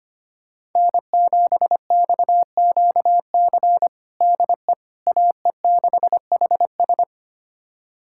Morse code, heard as a keyed tone, about 25 wpm, 700 Hz. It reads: N7XQC DE AE6HS
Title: USA Callsign DE Callsign - 5 Characters - 25wpm